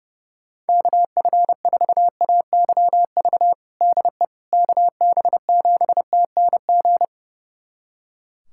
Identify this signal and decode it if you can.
Morse code, heard as a keyed tone, about 30 wpm, 700 Hz. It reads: KF4AYV DE KB7TNG
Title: USA Callsign DE Callsign - 6 Characters - 30wpm